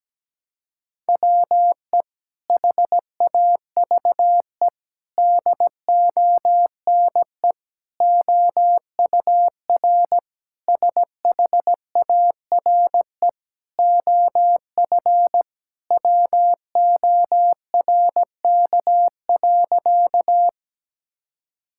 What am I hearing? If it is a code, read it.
Morse code, heard as a keyed tone, about 17 wpm, 700 Hz. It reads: WE HAVE DONE OUR SHARE OF WORK.
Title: Sentences from Top 1000 Words - Review Part 6 - 17wpm